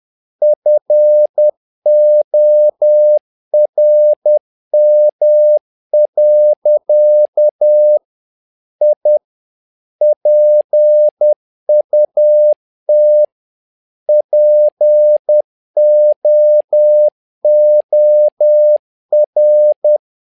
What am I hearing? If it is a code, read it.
Morse code, heard as a keyed tone, about 10 wpm, 600 Hz. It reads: FORM. I PUT POOR
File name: learning center / tower trouble / tower trouble 10wpm